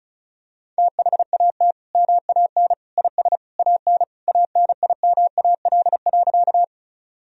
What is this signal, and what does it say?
Morse code, heard as a keyed tone, about 36 wpm, 700 Hz. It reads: THAT MAN IS AN ANIMAL.
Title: Sentences from Top 1000 Words - Review Part 1 - 35wpm